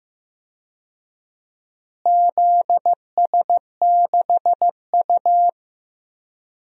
Morse code, heard as a keyed tone, about 15 wpm, 700 Hz. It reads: ZS6U
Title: DXCC-100 Call Signs - 4 Characters - Repeated 2x - 15wpm